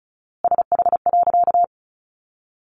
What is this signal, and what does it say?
Morse code, heard as a keyed tone, about 34 wpm, 700 Hz. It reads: SH.
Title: A Sound Of Thunder 35wpm